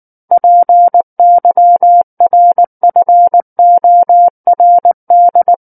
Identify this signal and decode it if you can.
Morse code, heard as a keyed tone, about 19 wpm, 700 Hz. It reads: PYRFORD
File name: Book One / 10-Chapter 10 - 19 WPM